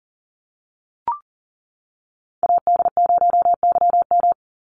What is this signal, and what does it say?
Morse code, heard as a keyed tone, about 40 wpm, 700 Hz. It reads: AD0YM
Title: NA SOTA Chaser Callsigns 40wpm